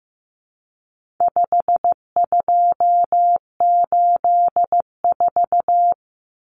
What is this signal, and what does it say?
Morse code, heard as a keyed tone, about 15 wpm, 700 Hz. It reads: 5284
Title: Common Cut Numbers - 4-Digit 15wpm